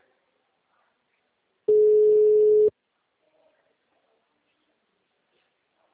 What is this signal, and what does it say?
Ringback tone (Europe, 425 Hz).
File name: out- -2017- - - .14